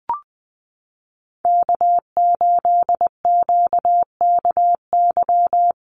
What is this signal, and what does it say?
Morse code, heard as a keyed tone, about 20 wpm, 700 Hz. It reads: K8QKY
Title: Call Signs Repeated 2x - 20wpm